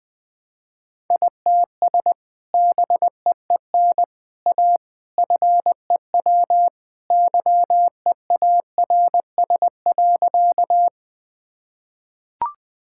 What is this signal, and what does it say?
Morse code, heard as a keyed tone, about 20 wpm, 700 Hz. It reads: ITS BEEN A FEW YEARS.
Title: Sentences with 3-Word Phrases - 20wpm